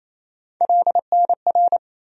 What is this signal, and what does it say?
Morse code, heard as a keyed tone, about 28 wpm, 700 Hz. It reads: LNR
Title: USA Call Sign Postfixes - 3 Letters 28wpm